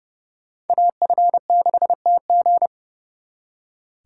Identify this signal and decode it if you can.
Morse code, heard as a keyed tone, about 38 wpm, 700 Hz. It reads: AF6TG
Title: Speed Racing - USA Call Signs 30wpm